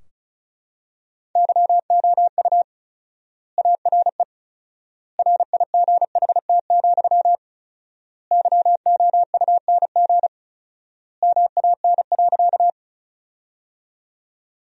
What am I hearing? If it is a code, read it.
Morse code, heard as a keyed tone, about 34 wpm, 700 Hz. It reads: YOU ARE RIGHT, YOUNG MAN.
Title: 4x Word-Spacing - Sentences from Top 400 Words - 35wpm